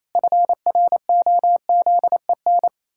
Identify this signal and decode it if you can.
Morse code, heard as a keyed tone, about 28 wpm, 700 Hz. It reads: FROZEN
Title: Pangram Challenge v2 - 28wpm